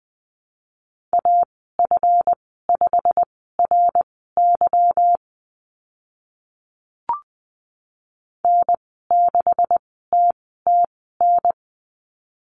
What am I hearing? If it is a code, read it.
Morse code, heard as a keyed tone, about 20 wpm, spaced as for 15 wpm, 700 Hz. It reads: AF5RY N6TTN
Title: 186 Lesson 30g - Letter b - Call Signs 20wpm